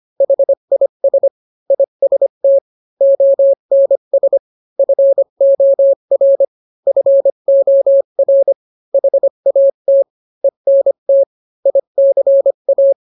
Morse code, heard as a keyed tone, about 25 wpm, 550 Hz. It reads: HIS IST ONS FOR FOR HAT ENT ICA